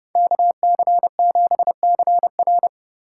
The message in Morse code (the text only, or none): KC7CR